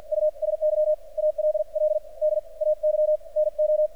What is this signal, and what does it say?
Morse code, heard as a keyed tone, about 46 wpm, spaced as for 38 wpm, 610 Hz. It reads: USY ALL DAY NO
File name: archived / CRSnet